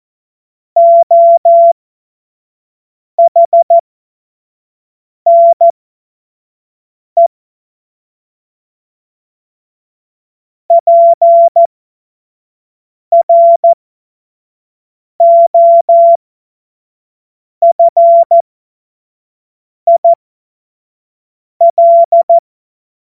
Morse code, heard as a keyed tone, about 14 wpm, spaced as for 5 wpm, 680 Hz. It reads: OHNE PROFIL
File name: Morsekurs / Witze / Witz 19. April 2022 25 F70